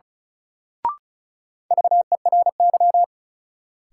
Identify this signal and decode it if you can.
Morse code, heard as a keyed tone, about 36 wpm, 700 Hz. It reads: VERY
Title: Sentences from Top 200 Words - New Word Drill - 35wpm